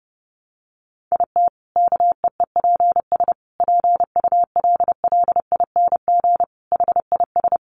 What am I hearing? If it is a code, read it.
Morse code, heard as a keyed tone, about 30 wpm, 700 Hz. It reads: IT KEEPS PULLING HIS
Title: Additional Easy Phrases 30wpm